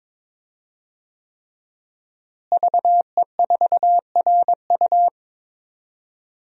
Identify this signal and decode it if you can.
Morse code, heard as a keyed tone, about 22 wpm, 700 Hz. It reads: VE4RU
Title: DXCC-100 Call Signs - 5 Characters - Repeated 3x - 22wpm